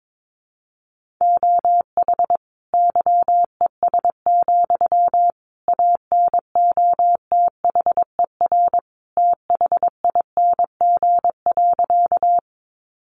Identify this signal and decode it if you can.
Morse code, heard as a keyed tone, about 22 wpm, 700 Hz. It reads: OH YES, ANOTHER THING.
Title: Sentences from Top 600 Words 22wpm